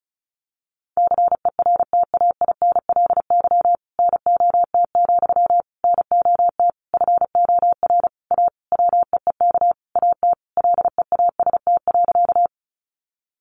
Morse code, heard as a keyed tone, about 34 wpm, 700 Hz. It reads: CERTAINLY NOT, NOT FOR A WEEK AT LEAST.
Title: Sentences from Top 900 Words 35wpm